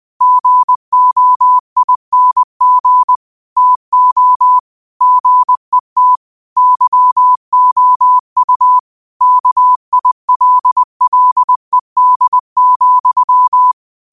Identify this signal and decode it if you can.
Morse code, heard as a keyed tone, about 20 wpm, 1000 Hz. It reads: GOING TO GET YOU KILLED,
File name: qotdfiles / 20wpm